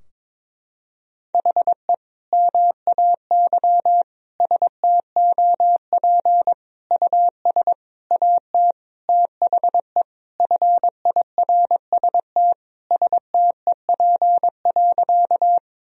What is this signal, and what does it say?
Morse code, heard as a keyed tone, about 22 wpm, 700 Hz. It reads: HE MAY STOP US AT THE FIRST STEP.
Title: Sentences from Top 1000 Words - Review Part 2 - 22wpm